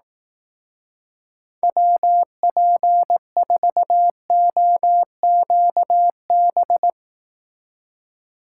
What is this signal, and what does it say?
Morse code, heard as a keyed tone, about 18 wpm, 700 Hz. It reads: WP4OQB
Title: Speed Racing - USA 2x3 Call Signs 15wpm